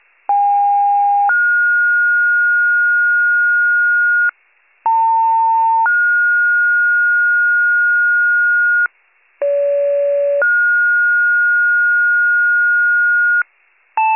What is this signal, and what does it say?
Two-tone page, heard as a one-second tone, then a three-second tone, about 810 then 1430 Hz, about 900 then 1430 Hz, about 570 then 1430 Hz. It is heard from 0.3 to 13.4 s.